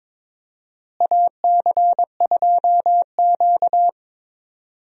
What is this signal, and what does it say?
Morse code, heard as a keyed tone, about 22 wpm, 700 Hz. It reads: AC2Q
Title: USA Call Signs - 4 Characters - Repeated 2x - 22wpm